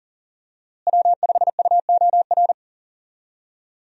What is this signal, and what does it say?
Morse code, heard as a keyed tone, about 39 wpm, 700 Hz. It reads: W5UOR
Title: USA Call Signs - 5 Characters - Repeated 3x - 40wpm